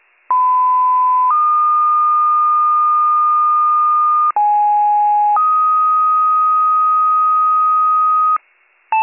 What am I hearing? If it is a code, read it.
Two-tone page, heard as a one-second tone, then a three-second tone, about 1010 then 1220 Hz, about 830 then 1250 Hz. It is heard from 0.3 to 8.4 s.